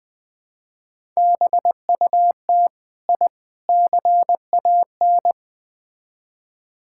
Morse code, heard as a keyed tone, about 20 wpm, 700 Hz. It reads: BUT I CAN
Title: Build A Phrase - 4-Word Phrases - 20wpm